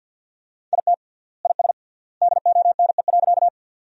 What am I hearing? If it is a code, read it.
Morse code, heard as a keyed tone, about 50 wpm, 700 Hz. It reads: IT IS DONE.
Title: 3x Word-Spacing - Sentences from Top 500 Words - 50wpm